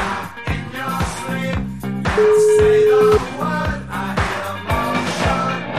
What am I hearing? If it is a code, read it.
Ringback tone (Europe, 425 Hz), with noise.